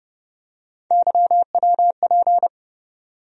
Morse code, heard as a keyed tone, about 31 wpm, 700 Hz. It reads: YWP
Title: USA Call Sign Postfixes - 3 Letters 30wpm